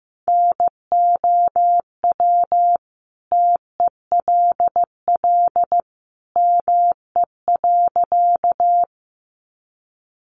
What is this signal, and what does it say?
Morse code, heard as a keyed tone, about 15 wpm, 700 Hz. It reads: NOW TELL ME.
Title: Sentences from Top 200 Words 15wpm